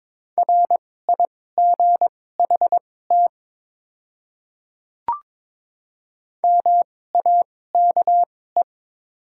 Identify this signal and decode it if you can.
Morse code, heard as a keyed tone, about 22 wpm, spaced as for 16 wpm, 700 Hz. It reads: RIGHT MAKE
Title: 2x Character Spacing - Top 200 Words - 22wpm